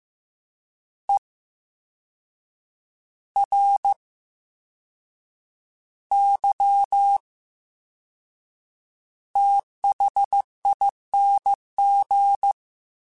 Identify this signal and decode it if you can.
Morse code, heard as a keyed tone, about 15 wpm, spaced as for 6 wpm, 800 Hz. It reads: ERYTHING
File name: Special INGs-mp3s / Special INGs-mp3s / VerySlowToFaster Ing @13